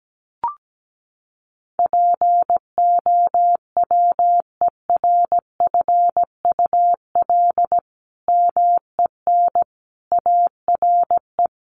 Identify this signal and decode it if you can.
Morse code, heard as a keyed tone, about 17 wpm, 700 Hz. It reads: POWERFUL MEN ARE
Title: Sentences from Top 1400 Words 17wpm